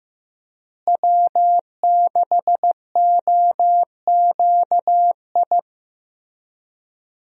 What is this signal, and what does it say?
Morse code, heard as a keyed tone, about 15 wpm, 700 Hz. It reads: W6OQI
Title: USA Call Signs - 5 Characters - Repeated 3x - 15wpm